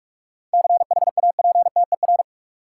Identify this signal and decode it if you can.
Morse code, heard as a keyed tone, about 43 wpm, 700 Hz. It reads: CHAPTER
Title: Seven Letter Words - 45wpm